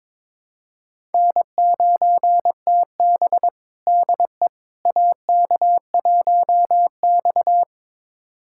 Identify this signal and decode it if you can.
Morse code, heard as a keyed tone, about 22 wpm, 700 Hz. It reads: N9TB DE AK1X
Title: USA Callsign DE Callsign - 4 Characters - 22wpm